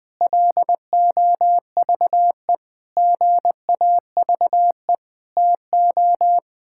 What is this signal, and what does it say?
Morse code, heard as a keyed tone, about 20 wpm, 700 Hz. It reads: LOVE GAVE TO
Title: Sets of 8 Words - 20wpm